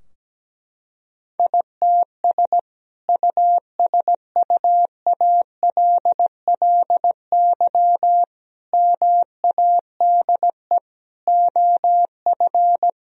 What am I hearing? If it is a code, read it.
Morse code, heard as a keyed tone, about 17 wpm, 700 Hz. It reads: ITS USUALLY MADE OF
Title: Sentences from Top 1000 Words - Review Part 3 - 17wpm